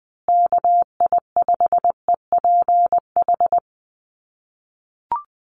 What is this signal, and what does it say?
Morse code, heard as a keyed tone, about 20 wpm, 700 Hz. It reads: KI5EPH
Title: USA Call Signs Repeated 2x 20wpm